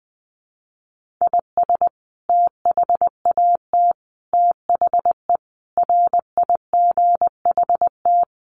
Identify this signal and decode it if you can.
Morse code, heard as a keyed tone, about 20 wpm, 700 Hz. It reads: IS THAT THE RIGHT
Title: Sentences from Top 1200 Words 20wpm